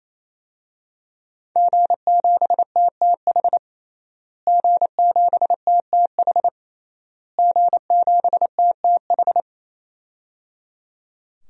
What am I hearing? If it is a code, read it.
Morse code, heard as a keyed tone, about 28 wpm, 700 Hz. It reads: G7TTH G7TTH G7TTH
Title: DXCC-100 Call Signs - 5 Characters - Repeated 3x - 28wpm